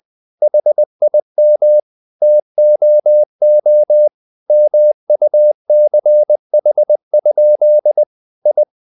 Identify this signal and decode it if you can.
Morse code, heard as a keyed tone, about 20 wpm, 600 Hz. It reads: HIM TOO MUCH? I
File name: the old man and the sea / the old man and the sea part 1 20wpm